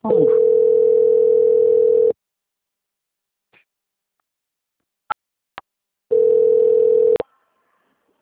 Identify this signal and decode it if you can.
Ringback tone (North America), 440 + 480 Hz, with speech.
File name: out- - - - - .10341